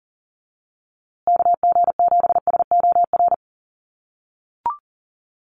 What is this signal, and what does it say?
Morse code, heard as a keyed tone, about 40 wpm, 700 Hz. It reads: KG7SOR
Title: USA Call Signs Repeated 2x 40wpm